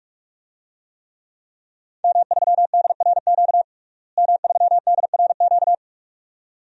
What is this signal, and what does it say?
Morse code, heard as a keyed tone, about 43 wpm, 700 Hz. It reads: M3DRQ M3DRQ
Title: DXCC-100 Call Signs - 5 Characters - Repeated 2x - 45wpm